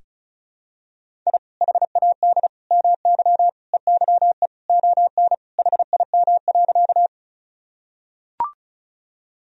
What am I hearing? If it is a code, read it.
Morse code, heard as a keyed tone, about 34 wpm, 700 Hz. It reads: I HAD MY EYE ON HIM.
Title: Sentences from Top 1000 Words - Review Part 1 - 35wpm